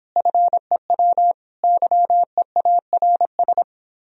Morse code, heard as a keyed tone, about 29 wpm, 700 Hz. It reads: FEW YEARS